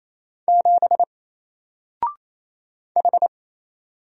Morse code, heard as a keyed tone, about 28 wpm, 700 Hz. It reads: 7 H
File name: Single-Letter-Number-28wpm